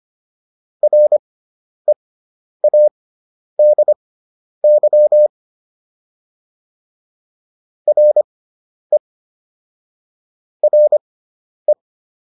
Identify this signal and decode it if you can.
Morse code, heard as a keyed tone, about 25 wpm, spaced as for 10 wpm, 600 Hz. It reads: READY RE RE